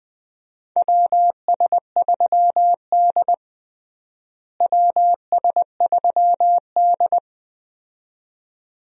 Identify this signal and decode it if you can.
Morse code, heard as a keyed tone, about 20 wpm, 700 Hz. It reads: WS3D WS3D
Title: USA Call Signs - 4 Characters - Repeated 2x - 20wpm